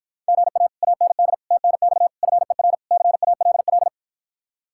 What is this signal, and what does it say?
Morse code, heard as a keyed tone, about 52 wpm, 700 Hz. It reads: GA AND TNX FER CALL